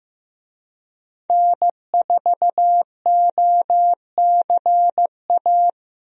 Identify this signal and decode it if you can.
Morse code, heard as a keyed tone, about 15 wpm, 700 Hz. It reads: N4OCA
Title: USA Call Signs - 5 Characters - Repeated 3x - 15wpm